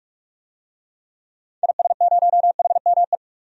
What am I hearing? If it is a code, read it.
Morse code, heard as a keyed tone, about 46 wpm, 700 Hz. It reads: IS0HME
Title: DXCC-100 Call Signs - 6 Characters - 45wpm